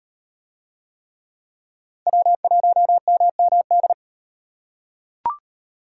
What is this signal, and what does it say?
Morse code, heard as a keyed tone, about 36 wpm, 700 Hz. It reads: W1MMD